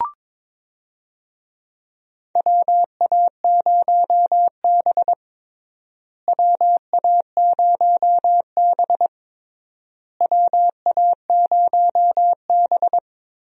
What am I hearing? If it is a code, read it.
Morse code, heard as a keyed tone, about 21 wpm, 700 Hz. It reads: WA0B WA0B WA0B
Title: USA Call Signs - 4 Characters - Repeated 3x - 22wpm